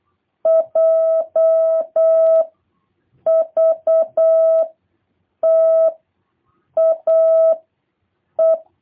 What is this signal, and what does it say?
Morse code, heard as a keyed tone, about 8 wpm, 650 Hz. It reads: J V T A E